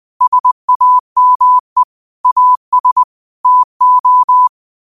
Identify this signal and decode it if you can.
Morse code, heard as a keyed tone, about 20 wpm, 1000 Hz. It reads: SAME AS TO